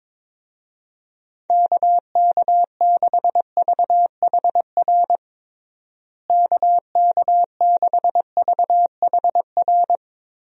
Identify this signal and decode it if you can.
Morse code, heard as a keyed tone, about 22 wpm, 700 Hz. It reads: KK6VHR KK6VHR
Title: USA Call Signs - 6 Characters - Repeated 3x - 22wpm